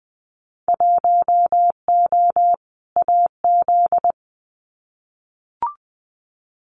Morse code, heard as a keyed tone, about 20 wpm, 700 Hz. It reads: 1O AZ
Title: Winter Field Day Exchanges 20wpm